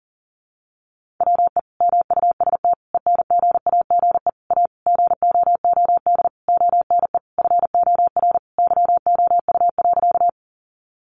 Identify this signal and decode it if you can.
Morse code, heard as a keyed tone, about 40 wpm, 700 Hz. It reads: WE MUST ENGAGE A GOOD ONE FOR YOU.
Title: Sentences from Top 1500 Words 40wpm